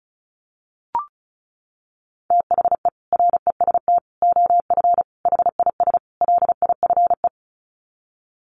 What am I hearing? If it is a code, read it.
Morse code, heard as a keyed tone, about 34 wpm, 700 Hz. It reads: THE REST OF HIS LIFE
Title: Sets of 5 Words 35wpm